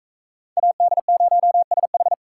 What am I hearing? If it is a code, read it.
Morse code, heard as a keyed tone, about 44 wpm, 700 Hz. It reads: AD0SH